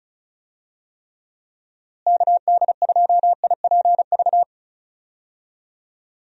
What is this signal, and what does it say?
Morse code, heard as a keyed tone, about 34 wpm, 700 Hz. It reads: KD2IPV
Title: USA Call Signs - 6 Characters - Repeated 3x - 35wpm